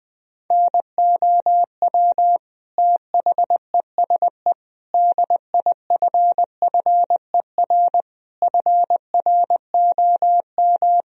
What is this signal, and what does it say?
Morse code, heard as a keyed tone, about 20 wpm, 700 Hz. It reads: NOW THESE DIFFER FROM
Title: Sentences from Top 1000 Words - Review Part 1 - 20wpm